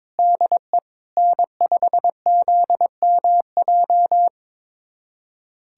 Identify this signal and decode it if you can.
Morse code, heard as a keyed tone, about 22 wpm, 700 Hz. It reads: DE N5ZMJ
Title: USA Callsign DE Callsign - 5 Characters - 22wpm